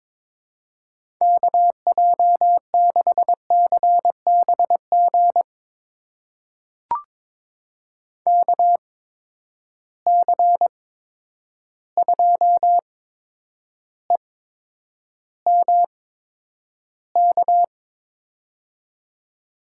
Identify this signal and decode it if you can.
Morse code, heard as a keyed tone, about 22 wpm, spaced as for 9 wpm, 700 Hz. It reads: KJ6CBG KC2EMK
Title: USA Call Signs - 6 Characters - 8x Character Spacing - 22wpm